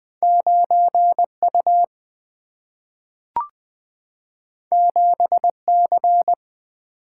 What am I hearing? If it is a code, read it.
Morse code, heard as a keyed tone, about 20 wpm, 700 Hz. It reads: 9 U 7 C
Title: Single-Letter Single-Number 20wpm